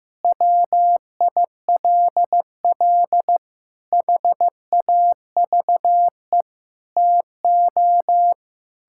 Morse code, heard as a keyed tone, about 15 wpm, 700 Hz. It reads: WILL HAVE TO
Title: Sentences from Top 1000 Words - Review Part 6 - 15wpm